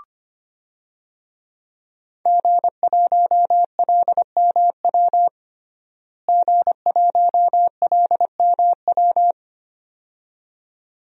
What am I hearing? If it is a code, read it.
Morse code, heard as a keyed tone, about 25 wpm, 700 Hz. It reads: G1LMW G1LMW
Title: DXCC-100 Call Signs - 5 Characters - Repeated 2x - 25wpm